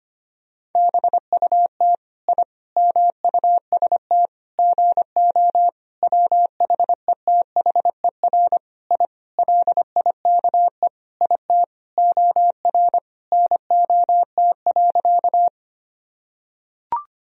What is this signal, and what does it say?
Morse code, heard as a keyed tone, about 25 wpm, 700 Hz. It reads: BUT I MUST GO WHETHER I LIKE IT OR NOT.
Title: Sentences from Top 1000 Words - Review Part 3 - 25wpm